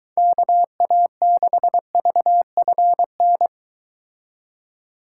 Morse code, heard as a keyed tone, about 23 wpm, 700 Hz. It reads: KA6VFN